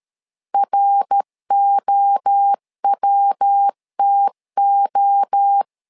Morse code, heard as a keyed tone, about 13 wpm, 800 Hz. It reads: ROWTO
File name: LEZIONI / 3 MWRO / 3 MWRO 12wpm 1730